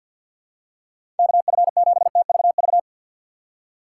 Morse code, heard as a keyed tone, about 51 wpm, 700 Hz. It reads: KF7TVV
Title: USA Call Signs - 3x Character Spacing - 50wpm